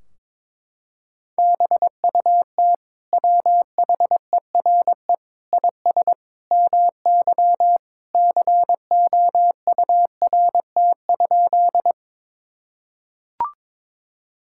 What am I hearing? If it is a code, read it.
Morse code, heard as a keyed tone, about 21 wpm, 700 Hz. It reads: BUT WHERE IS MY COURT?
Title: Sentences from Top 1600 Words 22wpm